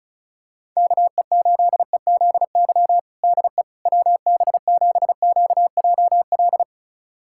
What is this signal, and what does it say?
Morse code, heard as a keyed tone, about 36 wpm, 700 Hz. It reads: KE8EZY DE WB7QJL
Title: USA Callsign DE Callsign - 6 Characters - 35wpm